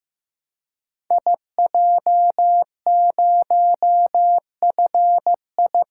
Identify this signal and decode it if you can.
Morse code, heard as a keyed tone, about 15 wpm, 700 Hz. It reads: IJ0FI
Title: Code Groups - 5 Letter:Numbers - 15wpm